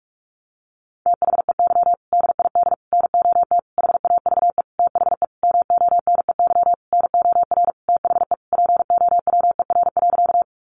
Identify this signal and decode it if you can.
Morse code, heard as a keyed tone, about 44 wpm, 700 Hz. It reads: THEY DID NOT HAVE THE MONEY NOR THE POWER.
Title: Sentences from Top 1500 Words 45wpm